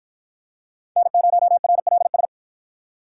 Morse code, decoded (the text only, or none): N0RLS